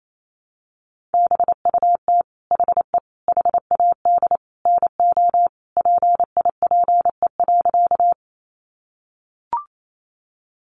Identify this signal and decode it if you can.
Morse code, heard as a keyed tone, about 28 wpm, 700 Hz. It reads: BUT HE HAD NO PIPE.
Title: Sentences from Top 1500 Words 28wpm